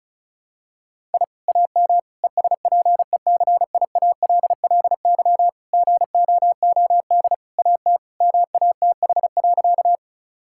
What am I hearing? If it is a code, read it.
Morse code, heard as a keyed tone, about 36 wpm, 700 Hz. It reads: I AM ESPECIALLY GOOD AT MATH.
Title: Sentences from Top 1000 Words - Review Part 6 - 35wpm